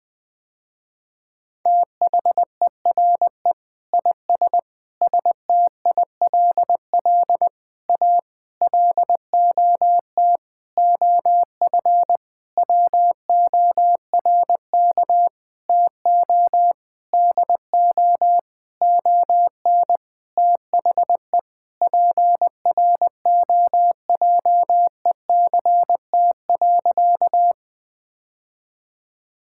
Morse code, heard as a keyed tone, about 20 wpm, 700 Hz. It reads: THERE IS STILL A LOT OF WORK TO DO ON THE PROJECT.
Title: Sentences with 9-Word Phrases - Spoken Beginning - 20wpm